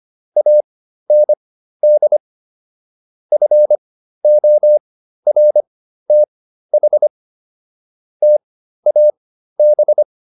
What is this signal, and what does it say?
Morse code, heard as a keyed tone, about 25 wpm, spaced as for 13 wpm, 600 Hz. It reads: AND FORTH TAB